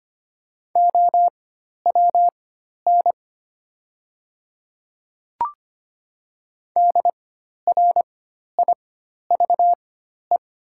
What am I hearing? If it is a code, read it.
Morse code, heard as a keyed tone, about 25 wpm, spaced as for 12 wpm, 700 Hz. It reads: OWN DRIVE
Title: 4x Character Spacing - Top 500 Words - 25wpm